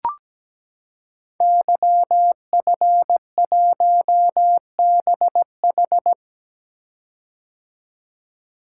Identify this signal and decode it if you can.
Morse code, heard as a keyed tone, about 17 wpm, 700 Hz. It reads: YF1BH
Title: Code Groups - 5 Letter:Numbers - 17wpm